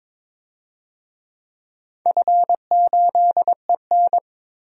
Morse code, heard as a keyed tone, about 22 wpm, 700 Hz. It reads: F8EN
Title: DXCC-100 Call Signs - 4 Characters - Repeated 2x - 22wpm